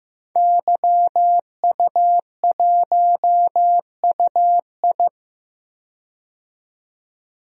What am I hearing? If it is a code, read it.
Morse code, heard as a keyed tone, about 16 wpm, 700 Hz. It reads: YU1UI
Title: DXCC-100 Call Signs - 5 Characters - 15wpm